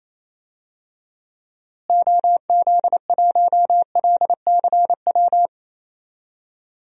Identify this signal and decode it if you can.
Morse code, heard as a keyed tone, about 27 wpm, 700 Hz. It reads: OZ1LCW
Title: DXCC-100 Call Signs - 6 Characters - Repeated 2x - 28wpm